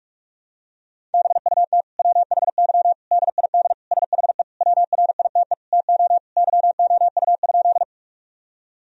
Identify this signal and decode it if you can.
Morse code, heard as a keyed tone, about 46 wpm, 700 Hz. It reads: BUT WHY DID SHE WRITE TO YOU?
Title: Sentences from Top 1000 Words - Review Part 1 - 45wpm